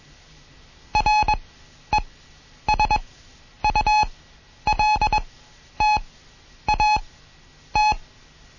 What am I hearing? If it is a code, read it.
Morse code, heard as a keyed tone, about 21 wpm, spaced as for 10 wpm, 850 Hz. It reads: RESULTAT
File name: Morsen / Kurs A 20